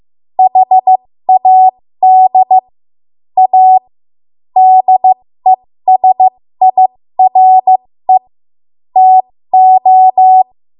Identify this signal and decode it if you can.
Morse code, heard as a keyed tone, about 15 wpm, spaced as for 13 wpm, 750 Hz. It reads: HAD A DESIRE TO
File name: Morse / 13WPM